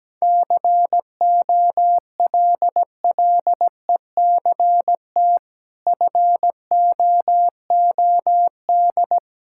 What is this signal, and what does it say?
Morse code, heard as a keyed tone, about 17 wpm, 700 Hz. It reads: COLLECT FOOD
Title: Sentences from Top 1600 Words 17wpm